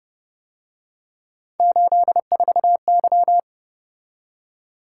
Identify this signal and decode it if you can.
Morse code, heard as a keyed tone, about 30 wpm, 700 Hz. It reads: 84Y
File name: Speed-Racing - Three Letter-Number Combinations - 25wpm